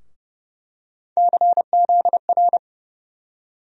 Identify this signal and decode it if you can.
Morse code, heard as a keyed tone, about 30 wpm, 700 Hz. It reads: CZR
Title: USA Call Sign Postfixes - 3 Letters 30wpm